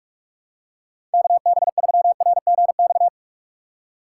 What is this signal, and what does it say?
Morse code, heard as a keyed tone, about 45 wpm, 700 Hz. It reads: KB3RGX
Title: USA Call Signs - 6 Characters - 4x Character Spacing - 45wpm